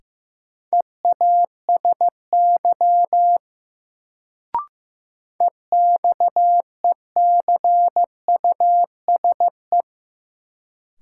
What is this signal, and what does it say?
Morse code, heard as a keyed tone, about 15 wpm, 700 Hz. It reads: EASY EXECUSE
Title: Sentences from Top 700 Words - New Word Drill - 15wpm